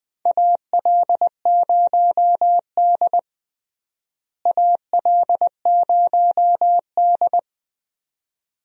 Morse code, heard as a keyed tone, about 20 wpm, 700 Hz. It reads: AL0D AL0D
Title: USA Call Signs - 4 Characters - Repeated 2x - 20wpm